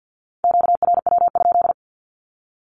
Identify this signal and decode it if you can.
Morse code, heard as a keyed tone, about 48 wpm, 700 Hz. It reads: QRW?
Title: Q-Codes Repeated 3x 50wpm